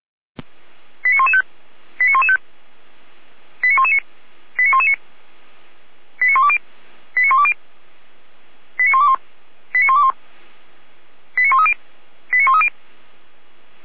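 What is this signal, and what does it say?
Five-tone selective call (ZVEI), heard as quick runs of five tones, tones of about 70 ms. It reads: 89115 89115 89119 89119 89120 89120 89121 89121 89130 89130
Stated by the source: ZVEI five-tone call 89099 received by SDR on a Feuerwehr Orlamünde server.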